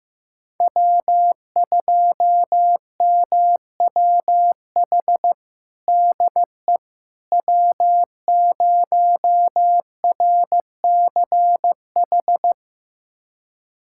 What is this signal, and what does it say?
Morse code, heard as a keyed tone, about 15 wpm, 700 Hz. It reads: W2MWH DE W0RCH
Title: USA Callsign DE Callsign - 5 Characters - 15wpm